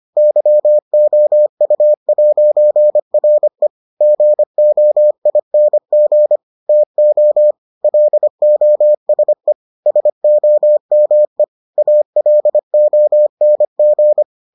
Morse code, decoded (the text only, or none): YOU'RE GOING TO LOSE SOME ALONG